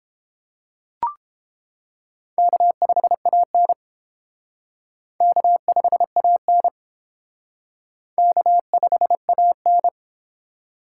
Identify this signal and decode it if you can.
Morse code, heard as a keyed tone, about 29 wpm, 700 Hz. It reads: K5AN K5AN K5AN